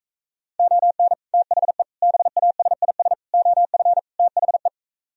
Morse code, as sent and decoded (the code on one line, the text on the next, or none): --- -. / - .... / -... .- ... .. ... / --- ..-. / - .... .
ON THE BASIS OF THE